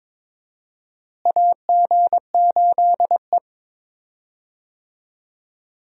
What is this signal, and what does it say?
Morse code, heard as a keyed tone, about 22 wpm, 700 Hz. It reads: AG8E
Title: USA Call Signs - 4 Characters - Repeated 2x - 22wpm